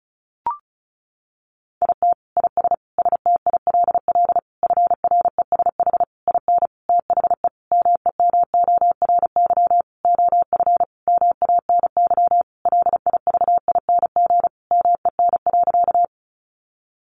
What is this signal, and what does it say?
Morse code, heard as a keyed tone, about 34 wpm, 700 Hz. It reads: IT IS STILL FRESH IN THE MEMORY OF MANY LIVING MEN.
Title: Sentences from Top 1300 Words 35wpm